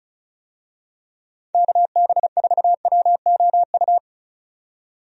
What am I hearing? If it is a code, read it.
Morse code, heard as a keyed tone, about 36 wpm, 700 Hz. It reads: KB4WOU
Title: USA Call Signs - 6 Characters - 35wpm